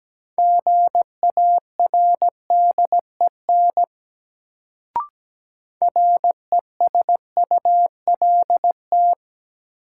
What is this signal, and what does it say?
Morse code, heard as a keyed tone, about 17 wpm, 700 Hz. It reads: GARDEN RESULT
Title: Sentences from Top 700 Words - New Word Drill - 17wpm